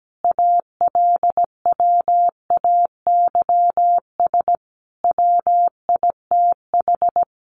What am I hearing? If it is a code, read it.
Morse code, heard as a keyed tone, about 17 wpm, 700 Hz. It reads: ALWAYS WITH
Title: Sentences from Top 400 Words 17wpm